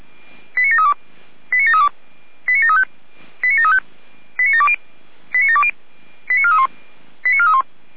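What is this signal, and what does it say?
Five-tone selective call (ZVEI), heard as quick runs of five tones, tones of about 70 ms. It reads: 89732 89732 89735 89735 89720 89720 89431 89431
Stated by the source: ZVEI five-tone selective call received by SDR.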